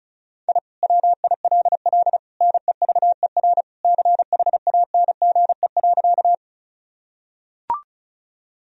Morse code, heard as a keyed tone, about 36 wpm, 700 Hz. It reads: I WILL NEVER CHANGE.
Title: Sentences from Top 1000 Words - Review Part 1 - 35wpm